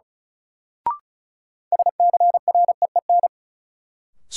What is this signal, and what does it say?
Morse code, heard as a keyed tone, about 36 wpm, 700 Hz. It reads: SCREEN
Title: Sentences from Top 1000 Words - Review Part 6 - New Word Drill - 35wpm